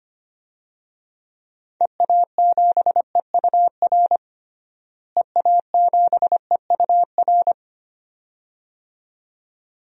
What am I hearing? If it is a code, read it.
Morse code, heard as a keyed tone, about 25 wpm, 700 Hz. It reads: EA7EUR EA7EUR
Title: DXCC-100 Call Signs - 6 Characters - Repeated 2x - 25wpm